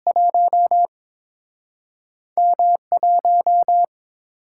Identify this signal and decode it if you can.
Morse code, heard as a keyed tone, about 30 wpm, spaced as for 26 wpm, 700 Hz. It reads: 1 M1
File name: Speed-Racing - Two Letter-Number Combinations - 22wpm